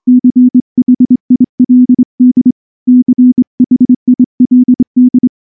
Morse code, heard as a keyed tone, about 24 wpm, 250 Hz. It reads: CHILD CHILD
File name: morse / morse-training-25WPM